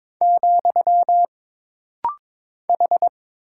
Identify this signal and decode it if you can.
Morse code, heard as a keyed tone, about 22 wpm, 700 Hz. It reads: , H
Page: This rapid-fire timing is geared to push your ICR (Instant Character Recognition) skills.